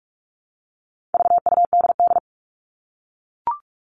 Morse code, heard as a keyed tone, about 44 wpm, 700 Hz. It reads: VUDD
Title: All Cut Numbers - 4-Digit 45wpm